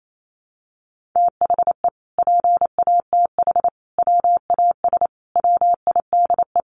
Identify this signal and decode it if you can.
Morse code, heard as a keyed tone, about 28 wpm, 700 Hz. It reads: THE PATH WAS WIDE
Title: Sentences from Top 1400 Words 28wpm